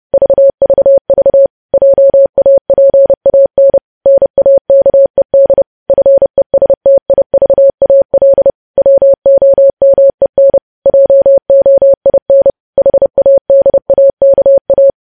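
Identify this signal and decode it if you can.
Morse code, heard as a keyed tone, about 30 wpm, 550 Hz. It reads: VVV JAPAN NAKED FESTIVAL WOMEN JOIN HADAKA
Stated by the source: Morse news headlines sent at 30 wpm, opening with vvv.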